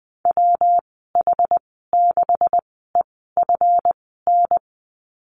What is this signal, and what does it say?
Morse code, heard as a keyed tone, about 20 wpm, spaced as for 14 wpm, 700 Hz. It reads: WH6EFN
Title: 147 Lesson 24g - Letter y - Call Signs 20wpm